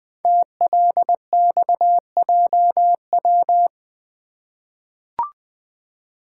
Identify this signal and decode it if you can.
Morse code, heard as a keyed tone, about 20 wpm, 700 Hz. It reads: TLXJW
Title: Code Groups - 5 Letters - 20wpm